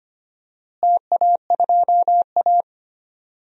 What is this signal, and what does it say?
Morse code, heard as a keyed tone, about 25 wpm, 700 Hz. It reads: TA2A
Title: DXCC-100 Call Signs - 4 Characters - Repeated 2x - 25wpm